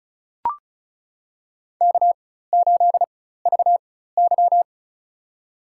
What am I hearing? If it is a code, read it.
Morse code, heard as a keyed tone, about 35 wpm, spaced as for 16 wpm, 700 Hz. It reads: K8VY
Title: USA Call Signs - 4 Characters - 4x Character Spacing - 35wpm